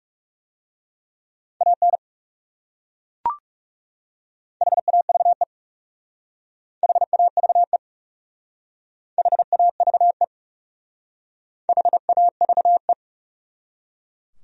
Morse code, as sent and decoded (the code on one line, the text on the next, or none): .- -. / .... .- ...- . / .... .- ...- . / .... .- ...- . / .... .- ...- .
AN HAVE HAVE HAVE HAVE